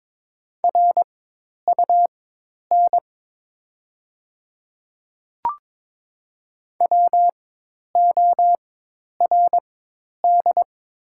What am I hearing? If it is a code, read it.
Morse code, heard as a keyed tone, about 22 wpm, spaced as for 10 wpm, 700 Hz. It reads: RUN WORD
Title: 4x Character Spacing - Top 500 Words - 22wpm